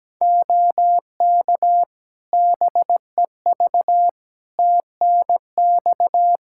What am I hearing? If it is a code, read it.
Morse code, heard as a keyed tone, about 17 wpm, 700 Hz. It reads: OK BEV TNX
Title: Practice QSO - Exchange 4 - Ending - 17wpm